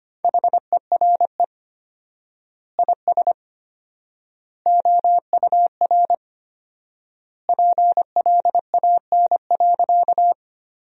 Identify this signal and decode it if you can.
Morse code, heard as a keyed tone, about 25 wpm, 700 Hz. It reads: HERE IS OUR PLAN.
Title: 4x Word-Spacing - Sentences from Top 500 Words - 25wpm